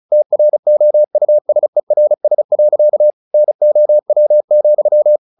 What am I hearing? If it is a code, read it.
Morse code, heard as a keyed tone, about 34 wpm, 600 Hz. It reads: TROUSERS. NOW,
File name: the old man and the sea / the old man and the sea 1 35wpm